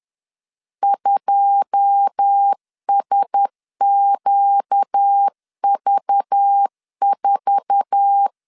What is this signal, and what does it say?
Morse code, heard as a keyed tone, about 10 wpm, 800 Hz. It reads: 2SQV4